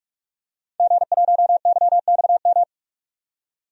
Morse code, heard as a keyed tone, about 44 wpm, 700 Hz. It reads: G1YXM
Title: DXCC-100 Call Signs - 5 Characters - 45wpm